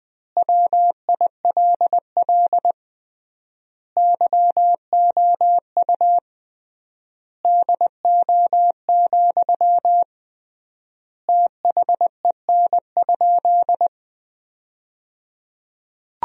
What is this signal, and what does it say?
Morse code, heard as a keyed tone, about 20 wpm, 700 Hz. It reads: WILL YOU DO, THEN?
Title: 3x Word-Spacing - Sentences from Top 100 Words - 20wpm